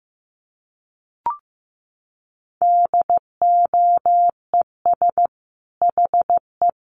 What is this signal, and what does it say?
Morse code, heard as a keyed tone, about 15 wpm, 700 Hz. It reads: DOES HE
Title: Sentences from Top 600 Words 15wpm